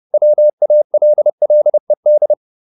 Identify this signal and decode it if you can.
Morse code, heard as a keyed tone, about 30 wpm, 600 Hz. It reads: WALLED